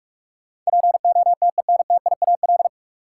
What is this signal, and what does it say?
Morse code, heard as a keyed tone, about 46 wpm, 700 Hz. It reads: POTENTIAL
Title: Sentences from Top 1000 Words - Review Part 4 - New Word Drill - 45wpm